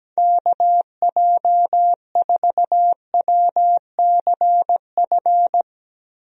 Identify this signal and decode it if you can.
Morse code, heard as a keyed tone, about 17 wpm, 700 Hz. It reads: KJ4WCF
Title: USA Call Signs - 6 Characters - Repeated 3x - 17wpm